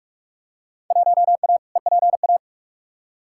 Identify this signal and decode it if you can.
Morse code, heard as a keyed tone, about 43 wpm, 700 Hz. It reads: 1A EPA
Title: Field Day Exchanges - 45wpm